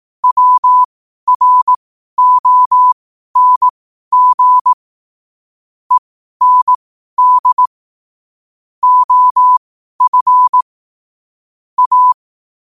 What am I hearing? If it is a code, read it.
Morse code, heard as a keyed tone, about 18 wpm, spaced as for 12 wpm, 1000 Hz. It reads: WRONG END OF A